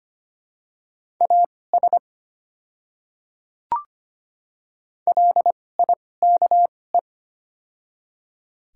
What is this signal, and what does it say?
Morse code, heard as a keyed tone, about 25 wpm, spaced as for 18 wpm, 690 Hz. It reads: AS LIKE
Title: 2x Character Spacing - Top 100 Words - 25wpm